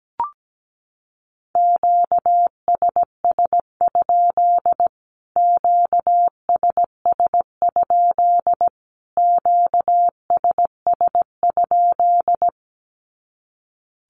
Morse code, heard as a keyed tone, about 17 wpm, 700 Hz. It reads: QSS? QSS? QSS?
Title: Q-Codes Repeated 3x 17wpm